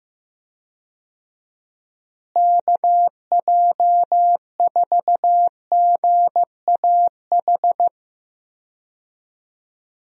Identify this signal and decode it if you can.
Morse code, heard as a keyed tone, about 15 wpm, 700 Hz. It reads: KJ4GAH
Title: USA Call Signs - 6 Characters - Repeated 3x - 15wpm